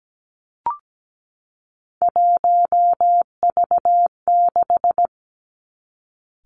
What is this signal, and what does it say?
Morse code, heard as a keyed tone, about 17 wpm, 700 Hz. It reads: 1V6
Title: Three Challenging Letter-Number Combinations 17wpm